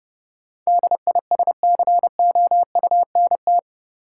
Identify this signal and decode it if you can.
Morse code, heard as a keyed tone, about 30 wpm, 700 Hz. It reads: DISCOUNT
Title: Sentences from Top 1200 Words - New Word Drill - 30wpm